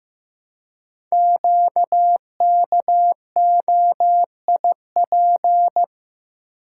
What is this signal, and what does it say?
Morse code, heard as a keyed tone, about 15 wpm, 700 Hz. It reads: QKOIP
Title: Code Groups - 5 Letters - 15wpm